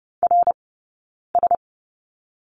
Morse code, heard as a keyed tone, about 30 wpm, 700 Hz. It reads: R S
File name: Lesson 19d - Letter w - Two Characters 30wpm